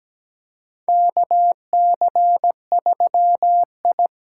Morse code, heard as a keyed tone, about 17 wpm, 700 Hz. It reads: KC3I
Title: USA Call Signs - 6 Characters - 8x Character Spacing - 17wpm